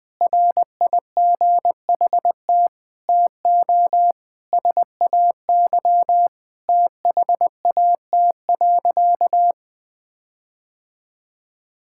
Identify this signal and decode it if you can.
Morse code, heard as a keyed tone, about 20 wpm, 700 Hz. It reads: RIGHT TO SAY THAT.
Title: Sentences with 5-Word Phrases - 20wpm